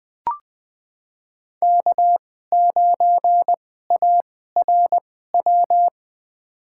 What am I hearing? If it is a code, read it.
Morse code, heard as a keyed tone, about 20 wpm, spaced as for 14 wpm, 700 Hz. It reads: K9ARW
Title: USA Call Signs - 5 Characters - 2x Character Spacing - 20wpm